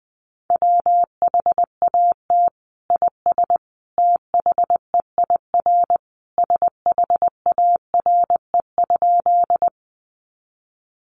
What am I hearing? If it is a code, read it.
Morse code, heard as a keyed tone, about 20 wpm, 700 Hz. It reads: WHAT IS THEIR SHARE?
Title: Sentences from Top 900 Words 20wpm